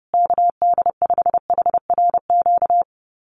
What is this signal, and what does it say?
Morse code, heard as a keyed tone, about 30 wpm, 700 Hz. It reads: KD5HRQ
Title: USA Call Signs Repeated 3x 30wpm